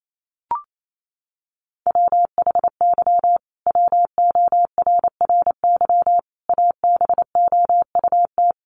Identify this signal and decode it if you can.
Morse code, heard as a keyed tone, about 28 wpm, 700 Hz. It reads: WHY WORRY ABOUT
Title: Sentences from Top 1100 Words 28wpm